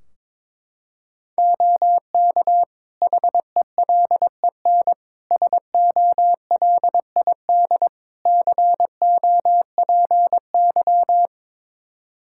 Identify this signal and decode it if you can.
Morse code, heard as a keyed tone, about 22 wpm, 700 Hz. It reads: OK HELEN SOLID COPY
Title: Practice QSO - Exchange 3 - More Info - 22wpm